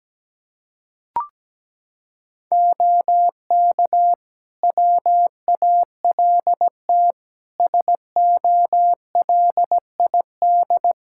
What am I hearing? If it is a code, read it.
Morse code, heard as a keyed tone, about 17 wpm, 700 Hz. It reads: OK WALT SOLID
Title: Practice QSO - Exchange 3 - More Info - 17wpm